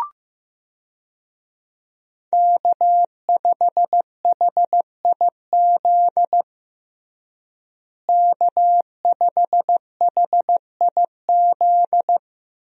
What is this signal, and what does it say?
Morse code, heard as a keyed tone, about 15 wpm, 700 Hz. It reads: K5HIZ K5HIZ
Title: USA Call Signs - 5 Characters - Repeated 2x - 15wpm